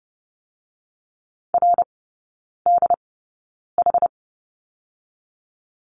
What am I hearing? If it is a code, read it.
Morse code, heard as a keyed tone, about 30 wpm, spaced as for 9 wpm, 700 Hz. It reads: RDH
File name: Lesson 16e - Letter c - Three Characters 30wpm